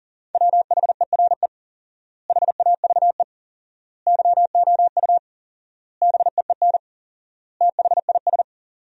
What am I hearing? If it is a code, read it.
Morse code, heard as a keyed tone, about 40 wpm, 700 Hz. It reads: WHERE HAVE YOU BEEN THIS
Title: 4x Word-Spacing - Sentences from Top 300 Words - 40wpm